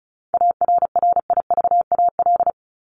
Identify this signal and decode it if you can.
Morse code, heard as a keyed tone, about 34 wpm, 700 Hz. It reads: ARRIVAL
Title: Top 1751-2000 Words 35wpm